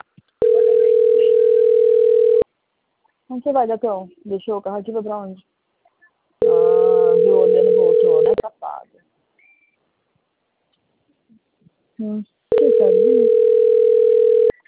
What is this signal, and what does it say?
Ringback tone (North America), 440 + 480 Hz, with speech.
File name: out- - - - - .81371